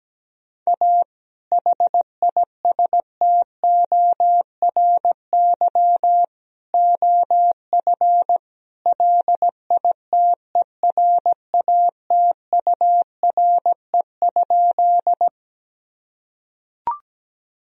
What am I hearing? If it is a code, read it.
Morse code, heard as a keyed tone, about 17 wpm, 700 Hz. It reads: A HISTORY OF LITERATURE?
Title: Sentences from Top 1000 Words - Review Part 3 - 17wpm